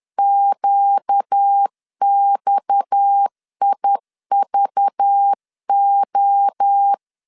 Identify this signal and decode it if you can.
Morse code, heard as a keyed tone, about 10 wpm, 800 Hz. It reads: QXIVO
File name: CW / DET / 1 6 1500C S B 10wpm 4137